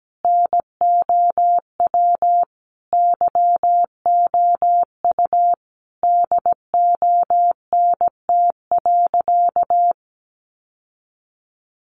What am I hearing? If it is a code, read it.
Morse code, heard as a keyed tone, about 17 wpm, 700 Hz. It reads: NOW YOU DONT.
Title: Sentences from Top 600 Words 17wpm